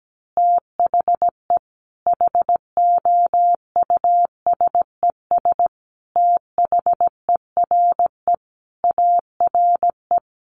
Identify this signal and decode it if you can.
Morse code, heard as a keyed tone, about 17 wpm, 700 Hz. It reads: THE HOUSES THERE ARE
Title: Easy Phrases 17wpm